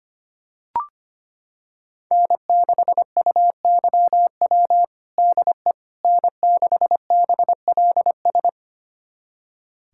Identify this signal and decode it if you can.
Morse code, heard as a keyed tone, about 25 wpm, 700 Hz. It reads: N6UYW DE N6BLS
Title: USA Callsign DE Callsign - 5 Characters - 25wpm